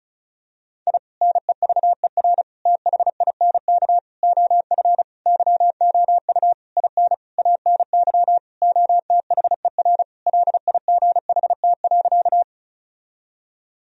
Morse code, heard as a keyed tone, about 34 wpm, 700 Hz. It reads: I NEVER THINK OF YOU IN ANY OTHER LIGHT.
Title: Sentences from Top 1000 Words - Review Part 1 - 35wpm